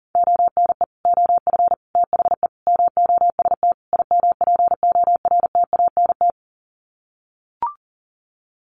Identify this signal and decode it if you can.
Morse code, heard as a keyed tone, about 40 wpm, 700 Hz. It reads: ONE OF THE MOST IMPORTANT
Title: Sets of 5 Words 40wpm